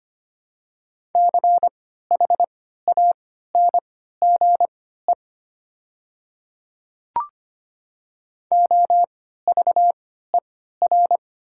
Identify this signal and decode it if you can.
Morse code, heard as a keyed tone, about 25 wpm, spaced as for 14 wpm, 690 Hz. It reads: CHANGE OVER
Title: 3x Character Spacing - Top 200 Words - 25wpm